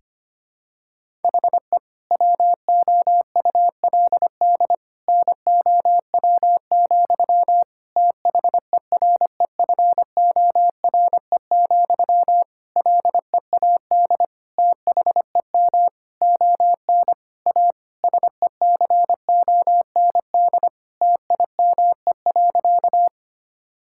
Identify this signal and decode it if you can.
Morse code, heard as a keyed tone, about 25 wpm, 700 Hz. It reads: HE WOULD NOW, THEREFORE, LEAD THEM ON A SECOND TIME.
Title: Sentences from Top 1000 Words - Review Part 5 - 25wpm